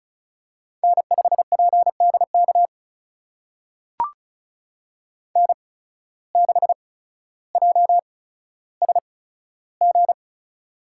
Morse code, heard as a keyed tone, about 34 wpm, spaced as for 15 wpm, 700 Hz. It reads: N5PDK N6JSG
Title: USA Call Signs - 8x Character Spacing - 35wpm